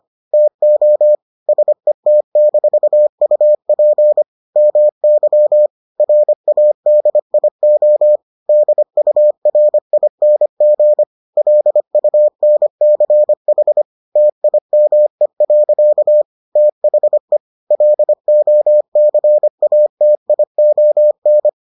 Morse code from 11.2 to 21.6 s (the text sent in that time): LUNCH TIME. THE LOCATION